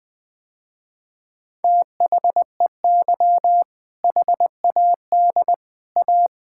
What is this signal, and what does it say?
Morse code, heard as a keyed tone, about 21 wpm, 700 Hz. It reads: THEY HAD A
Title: Sentences with 2-Word Phrases - Spoken Beginning - 20wpm